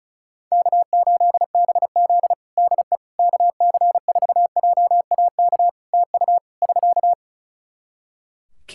Morse code, heard as a keyed tone, about 36 wpm, 700 Hz. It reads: K8BZ DE KC4JAK TU <SK>
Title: Practice QSO - Exchange 4 - Ending - 35wpm